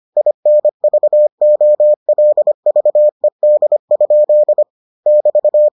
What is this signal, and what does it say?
Morse code, heard as a keyed tone, about 25 wpm, 600 Hz. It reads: INVOLVED? <BT>